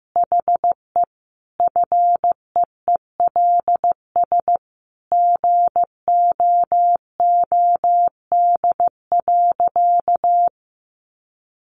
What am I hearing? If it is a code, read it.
Morse code, heard as a keyed tone, about 15 wpm, 700 Hz. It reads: HE FEELS GOOD.
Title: Additional Easy Phrases 15wpm